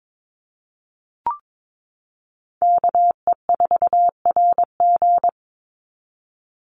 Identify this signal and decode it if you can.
Morse code, heard as a keyed tone, about 22 wpm, 700 Hz. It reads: KE4RG
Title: Call Signs - 22wpm